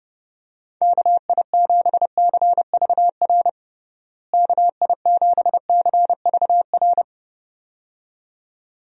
Morse code, heard as a keyed tone, about 32 wpm, 700 Hz. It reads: KI7CVR KI7CVR
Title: USA Call Signs - 6 Characters - Repeated 2x - 30wpm